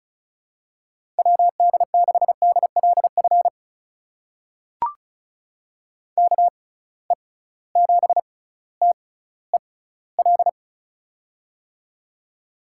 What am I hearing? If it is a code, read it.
Morse code, heard as a keyed tone, about 36 wpm, spaced as for 18 wpm, 700 Hz. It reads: WD6DLF KE7TEL
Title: USA Call Signs - 6x Character Spacing - 35wpm